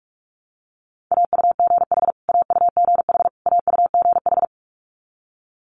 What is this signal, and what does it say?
Morse code, heard as a keyed tone, about 46 wpm, 700 Hz. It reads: AUGH AUGH AUGH
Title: Multi-letter Phonemes Repeated 3x 45wpm